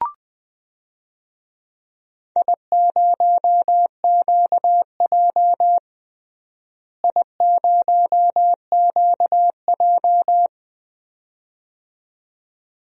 Morse code, heard as a keyed tone, about 20 wpm, 700 Hz. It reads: I0QJ I0QJ
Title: DXCC-100 Call Signs - 4 Characters - Repeated 2x - 20wpm